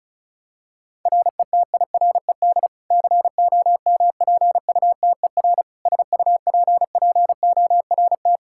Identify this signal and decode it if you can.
Morse code, heard as a keyed tone, about 34 wpm, 700 Hz. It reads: RETIRED COMPUTER SUPPORT
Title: Practice QSO - Exchange 3 - More Info - 35wpm